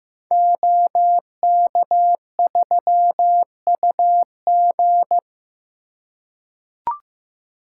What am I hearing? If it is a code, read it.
Morse code, heard as a keyed tone, about 15 wpm, 700 Hz. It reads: OK3UG
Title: Code Groups - 5 Letter:Numbers - 15wpm